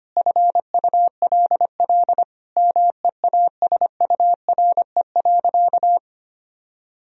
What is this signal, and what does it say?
Morse code, heard as a keyed tone, about 25 wpm, 700 Hz. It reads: FULL MEASURE.
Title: Sentences from Top 1000 Words - Review Part 3 - 25wpm